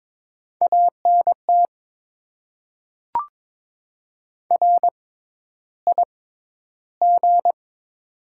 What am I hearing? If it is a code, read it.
Morse code, heard as a keyed tone, about 22 wpm, spaced as for 11 wpm, 700 Hz. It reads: ANT RIG
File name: Top 20 QSO Elements - 6x Character Spacing - 22wpm